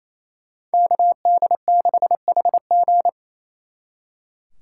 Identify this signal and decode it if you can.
Morse code, heard as a keyed tone, about 35 wpm, 700 Hz. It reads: KD6HG